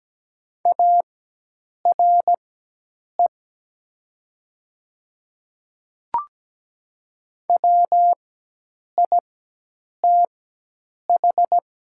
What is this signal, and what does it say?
Morse code, heard as a keyed tone, about 17 wpm, spaced as for 8 wpm, 700 Hz. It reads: ARE WITH
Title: 4x Character Spacing - Top 100 Words - 17wpm